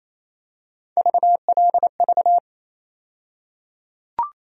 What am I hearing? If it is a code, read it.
Morse code, heard as a keyed tone, about 27 wpm, 700 Hz. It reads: VLV
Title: USA Call Sign Postfixes - 3 Letters 28wpm